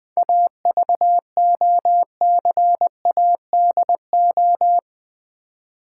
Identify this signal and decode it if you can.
Morse code, heard as a keyed tone, about 20 wpm, 700 Hz. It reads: AVOCADO
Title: Difficult Words with Hints - 20wpm